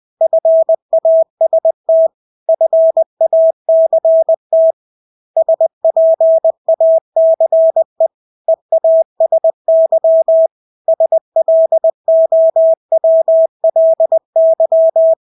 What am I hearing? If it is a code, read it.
Morse code, heard as a keyed tone, about 20 wpm, 650 Hz. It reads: FAST FACT SPACE EASY SLOWLY
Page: Engelske ord 20 wpm